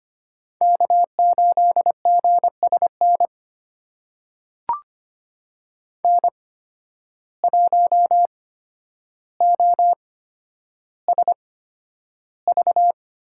Morse code, heard as a keyed tone, about 25 wpm, spaced as for 11 wpm, 700 Hz. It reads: K8GSN N1OSV
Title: USA Call Signs - 5 Characters - 8x Character Spacing - 25wpm